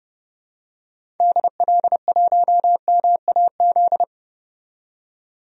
Morse code, heard as a keyed tone, about 30 wpm, 700 Hz. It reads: DL1MAZ
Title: DXCC-100 Call Signs - 6 Characters - 30wpm